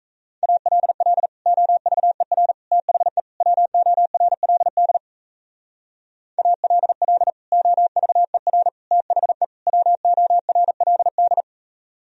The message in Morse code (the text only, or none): ALL OVER THE WORLD ALL OVER THE WORLD